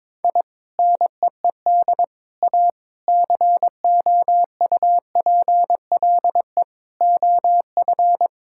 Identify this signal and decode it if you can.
Morse code, heard as a keyed tone, about 22 wpm, 700 Hz. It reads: I NEED A COUPLE OF
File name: practice / 22wpm / Sentences with 3-Word Phrases - 22wpm